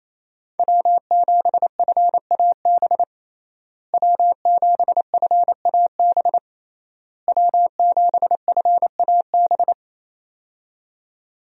Morse code, heard as a keyed tone, about 28 wpm, 700 Hz. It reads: W7FAB W7FAB W7FAB
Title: USA Call Signs - 5 Characters - Repeated 3x - 28wpm